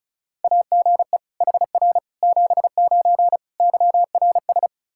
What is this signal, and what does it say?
Morse code, heard as a keyed tone, about 36 wpm, 700 Hz. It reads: AGE HR 79 YRS
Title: Practice QSO - Exchange 3 - More Info - 35wpm